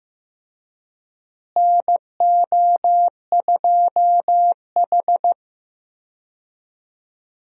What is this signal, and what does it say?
Morse code, heard as a keyed tone, about 15 wpm, 700 Hz. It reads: NO2H
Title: USA Call Signs - 4 Characters - Repeated 2x - 15wpm